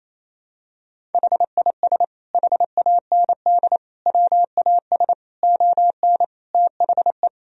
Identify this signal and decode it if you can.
Morse code, heard as a keyed tone, about 28 wpm, 700 Hz. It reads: HIS HAND WAS ON THE
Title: Sentences from Top 1000 Words - Review Part 1 - 28wpm